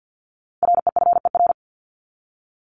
Morse code, heard as a keyed tone, about 50 wpm, 700 Hz. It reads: REFER
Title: Top 751-1000 Words 50wpm